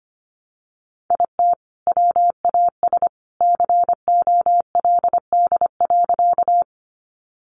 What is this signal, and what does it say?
Morse code, heard as a keyed tone, about 25 wpm, 700 Hz. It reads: IT WAS COLD.
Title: Sentences from Top 400 Words 25wpm